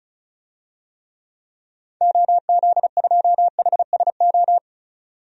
Morse code, heard as a keyed tone, about 36 wpm, 700 Hz. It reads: OZ2HSO
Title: DXCC-100 Call Signs - 6 Characters - 35wpm